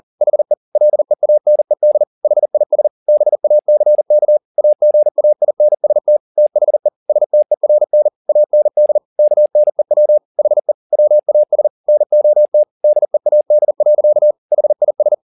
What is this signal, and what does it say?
Morse code, heard as a keyed tone, about 40 wpm, 600 Hz. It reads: HE LEANED HIS BACK AGAINST THE STERN AND KNEW HE WAS NOT DEAD. HIS
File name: the old man and the sea 2 40wpm